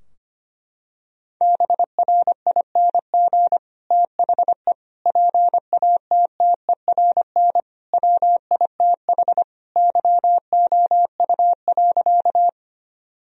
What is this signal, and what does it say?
Morse code, heard as a keyed tone, about 25 wpm, 700 Hz. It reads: BRING THE PATTERN WITH YOU.
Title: Sentences from Top 1000 Words - Review Part 2 - 25wpm